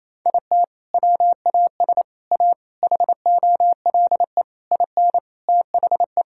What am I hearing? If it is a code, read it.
Morse code, heard as a keyed tone, about 28 wpm, 700 Hz. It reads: IT WAS A HOLE IN THE
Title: Sentences from Top 1600 Words 28wpm